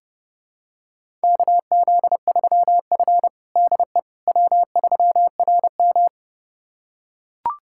Morse code, heard as a keyed tone, about 30 wpm, 700 Hz. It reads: KZ3F DE W3RM
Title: USA Callsign DE Callsign - 4 Characters - 30wpm